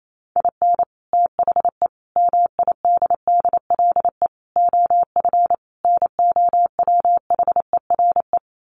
Morse code, heard as a keyed tone, about 28 wpm, 700 Hz. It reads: IN THE MIDDLE OF NOWHERE
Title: Sets of 5 Words Encore 28wpm